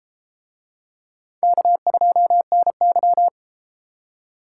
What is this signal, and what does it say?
Morse code, heard as a keyed tone, about 32 wpm, 700 Hz. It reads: K2NY